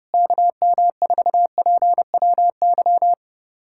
Morse code, heard as a keyed tone, about 30 wpm, 700 Hz. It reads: KM4PWY
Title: USA Callsign DE Callsign 30wpm